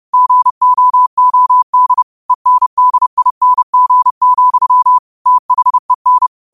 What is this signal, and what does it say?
Morse code, heard as a keyed tone, about 30 wpm, 1000 Hz. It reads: GOOD ENDING, THEN